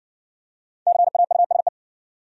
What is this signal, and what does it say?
Morse code, heard as a keyed tone, about 60 wpm, 700 Hz. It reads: CAUSE